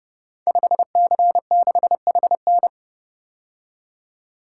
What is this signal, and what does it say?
Morse code, heard as a keyed tone, about 30 wpm, 700 Hz. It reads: 5C6HN
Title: Code Groups - 5 Letter:Numbers - 30wpm